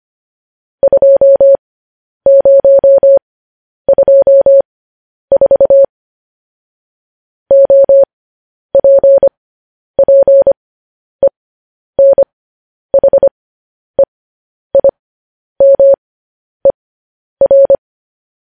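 Morse code, heard as a keyed tone, about 25 wpm, spaced as for 10 wpm, 550 Hz. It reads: 2024 OPPENHEIMER
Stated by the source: Morse code news headlines practice transmission.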